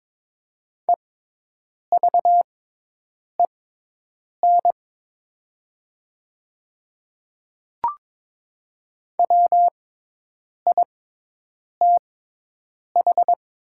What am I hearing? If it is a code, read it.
Morse code, heard as a keyed tone, about 22 wpm, spaced as for 8 wpm, 700 Hz. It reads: EVEN WITH
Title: 6x Character Spacing - Top 200 Words - 22wpm